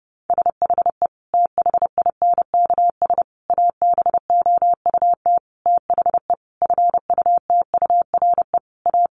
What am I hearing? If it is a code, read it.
Morse code, heard as a keyed tone, about 30 wpm, 700 Hz. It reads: SHE THINKS ABOUT THE FUTURE A
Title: Easy Phrases 30wpm